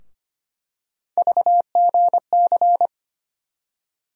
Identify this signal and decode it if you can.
Morse code, heard as a keyed tone, about 25 wpm, 700 Hz. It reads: VGC
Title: USA Call Sign Postfixes - 3 Letters 25wpm